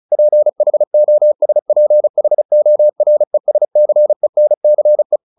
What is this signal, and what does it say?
Morse code, heard as a keyed tone, about 34 wpm, 600 Hz. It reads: PHOSPHORESCENCE
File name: the old man and the sea 2 35wpm